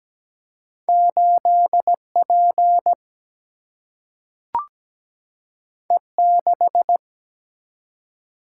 Morse code, heard as a keyed tone, about 17 wpm, 700 Hz. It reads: 8 P E 6
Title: Single-Letter Single-Number 17wpm